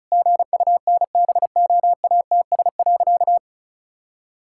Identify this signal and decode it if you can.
Morse code, heard as a keyed tone, about 34 wpm, 700 Hz. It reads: GUNBOATS.
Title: Pangram Challenge v2 - 35wpm